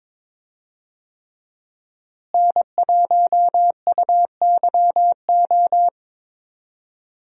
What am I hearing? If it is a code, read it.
Morse code, heard as a keyed tone, about 22 wpm, 700 Hz. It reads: N1UYO
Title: USA Call Signs - 5 Characters - Repeated 3x - 22wpm